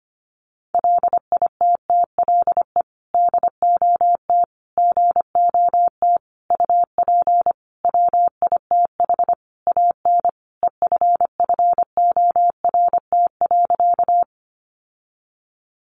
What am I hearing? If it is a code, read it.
Morse code, heard as a keyed tone, about 25 wpm, 700 Hz. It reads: LITTLE DOT GOT UP WITH AN EFFORT.
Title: Sentences from Top 1500 Words 25wpm